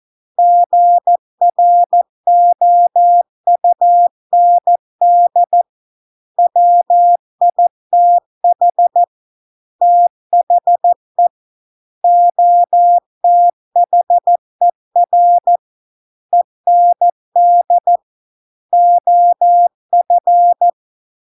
Morse code, heard as a keyed tone, about 14 wpm, 700 Hz. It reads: GROUND WITH THE OTHER END OF